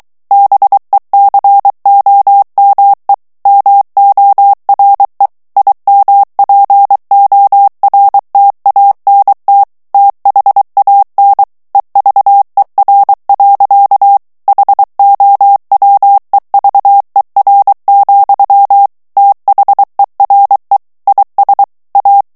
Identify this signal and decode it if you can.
Morse code, heard as a keyed tone, about 23 wpm, 800 Hz. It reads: BECOME MORE IMPORTANT THAN EVER. HOWEVER, THERE IS A
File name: morse mp3 / ANGLAIS texte clair 1 vitesse 22 mots minute